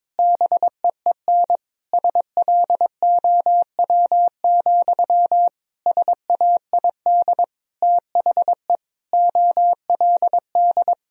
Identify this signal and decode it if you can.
Morse code, heard as a keyed tone, about 21 wpm, 700 Hz. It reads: BEEN SLOW, SAID THE OLD
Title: Sentences from Top 1000 Words - Review Part 2 - 22wpm